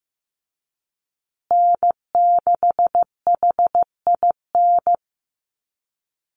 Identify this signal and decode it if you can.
Morse code, heard as a keyed tone, about 15 wpm, 700 Hz. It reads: N6HIN
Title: USA Call Signs Repeated 2x 15wpm